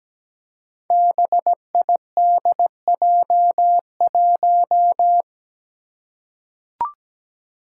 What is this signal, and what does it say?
Morse code, heard as a keyed tone, about 17 wpm, 700 Hz. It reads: BIDJ1
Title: Five Challenging Letter-Number Combinations 17wpm